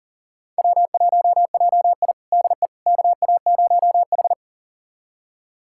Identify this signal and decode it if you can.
Morse code, heard as a keyed tone, about 40 wpm, 700 Hz. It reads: W1JI DE KA0H
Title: USA Callsign DE Callsign - 4 Characters - 40wpm